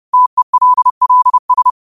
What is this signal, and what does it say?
Morse code, heard as a keyed tone, about 30 wpm, 1000 Hz. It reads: TELLS